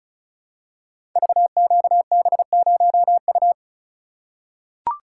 Morse code, heard as a keyed tone, about 36 wpm, 700 Hz. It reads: VQB0U
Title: Code Groups - 5 Letter:Numbers - 35wpm